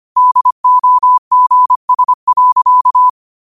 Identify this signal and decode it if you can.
Morse code, heard as a keyed tone, about 25 wpm, 1000 Hz. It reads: DOGS.